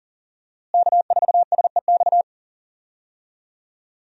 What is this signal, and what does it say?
Morse code, heard as a keyed tone, about 40 wpm, 700 Hz. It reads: K4SEX
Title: USA Call Signs - 5 Characters - Repeated 3x - 40wpm